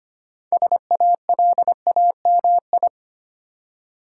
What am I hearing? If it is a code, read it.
Morse code, heard as a keyed tone, about 25 wpm, 700 Hz. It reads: SALAMI
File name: practice / 25wpm / Difficult Words with Hints - 25wpm